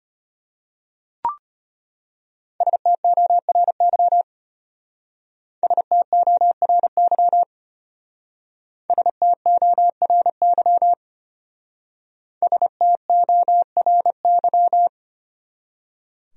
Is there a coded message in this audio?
Yes: Morse code from 2.6 to 14.9 s, heard as a keyed tone, about 34 wpm, 700 Hz. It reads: STORY STORY STORY STORY